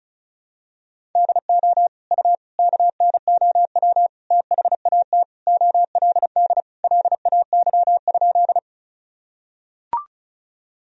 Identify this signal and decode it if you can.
Morse code, heard as a keyed tone, about 36 wpm, 700 Hz. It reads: DO U KNOW THAT OLD LAY?
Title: Sentences from Top 400 Words and Abbreviations 35wpm